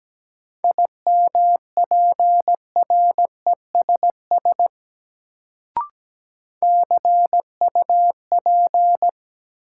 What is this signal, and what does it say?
Morse code, heard as a keyed tone, about 17 wpm, 700 Hz. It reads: IMPRESS CUP
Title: Sentences from Top 1600 Words - New Word Drill Alt - 17wpm